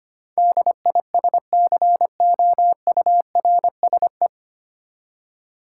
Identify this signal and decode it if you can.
Morse code, heard as a keyed tone, about 25 wpm, 700 Hz. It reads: DISCOURSE
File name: Nine Letter Words - 25wpm